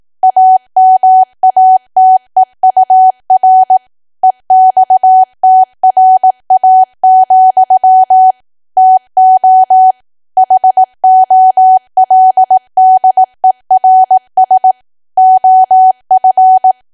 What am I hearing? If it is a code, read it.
Morse code, heard as a keyed tone, about 18 wpm, 750 Hz. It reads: AMATEUR EXTRA, TO HOLDERS OF
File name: cwa / W1AW-Level2 / 18WPM